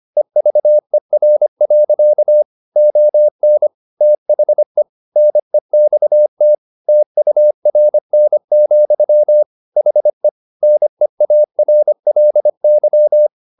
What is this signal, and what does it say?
Morse code, heard as a keyed tone, about 25 wpm, 600 Hz. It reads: EVER. ON THE NEXT TURN, HE NEARLY